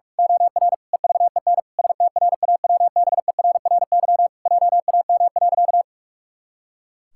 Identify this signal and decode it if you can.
Morse code, heard as a keyed tone, about 41 wpm, 700 Hz. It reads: OR EVEN STRAWBERRY JAM.
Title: Pangram Challenge v2 - 45wpm